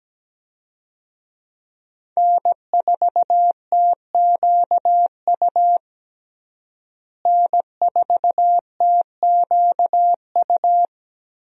Morse code, heard as a keyed tone, about 17 wpm, 700 Hz. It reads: N4TQU N4TQU
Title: USA Call Signs - 5 Characters - Repeated 2x - 17wpm